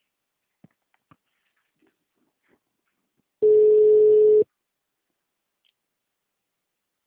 Ringback tone (Europe, 425 Hz).